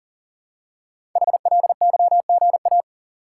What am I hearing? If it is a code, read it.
Morse code, heard as a keyed tone, about 38 wpm, 700 Hz. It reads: HLYGA
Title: Code Groups - 5 Letters - 40wpm